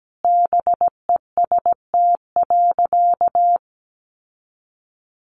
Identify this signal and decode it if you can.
Morse code, heard as a keyed tone, about 17 wpm, 700 Hz. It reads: BEST.
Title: Sentences from Top 1000 Words 17wpm